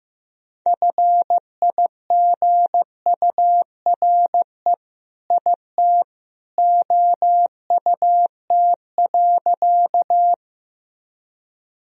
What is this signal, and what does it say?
Morse code, heard as a keyed tone, about 16 wpm, 700 Hz. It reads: FIGURE IT OUT.
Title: Sentences from Top 1000 Words - Review Part 2 - 15wpm